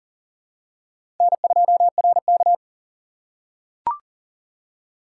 Morse code, heard as a keyed tone, about 38 wpm, 700 Hz. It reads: N2RK
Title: USA Call Signs - 4 Characters - 6x Character Spacing - 40wpm